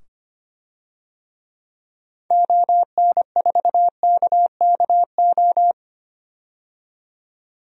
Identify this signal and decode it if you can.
Morse code, heard as a keyed tone, about 25 wpm, 700 Hz. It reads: ON4KKO
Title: DXCC-100 Call Signs - 6 Characters - 25wpm